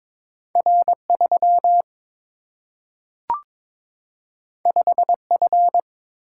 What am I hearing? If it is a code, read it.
Morse code, heard as a keyed tone, about 22 wpm, 700 Hz. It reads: R 3 5 F
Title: Single-Letter Single-Number 22wpm